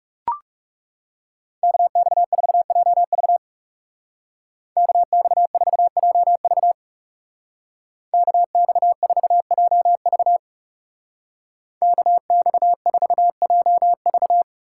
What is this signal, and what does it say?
Morse code, heard as a keyed tone, about 38 wpm, 700 Hz. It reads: KX4JV KX4JV KX4JV KX4JV